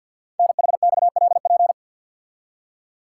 Morse code, heard as a keyed tone, about 52 wpm, 700 Hz. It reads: NHXLP
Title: Code Groups - 5 Letter:Numbers - 50wpm